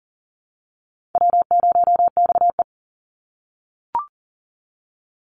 Morse code, heard as a keyed tone, about 40 wpm, 700 Hz. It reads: W0XE
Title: Call Signs - 40wpm